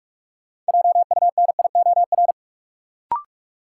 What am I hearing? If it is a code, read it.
Morse code, heard as a keyed tone, about 46 wpm, 700 Hz. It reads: JUNIOR
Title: Sentences from Top 1000 Words - Review Part 7 - New Word Drill - 45wpm